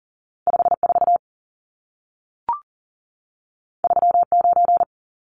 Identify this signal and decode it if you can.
Morse code, heard as a keyed tone, about 40 wpm, 700 Hz. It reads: 54 39
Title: Numbers - 2-Digit 40wpm